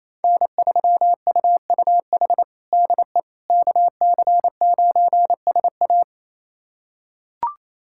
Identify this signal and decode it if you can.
Morse code, heard as a keyed tone, about 28 wpm, 700 Hz. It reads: N3UUH DE KC9SA
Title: USA Callsign DE Callsign - 5 Characters - 28wpm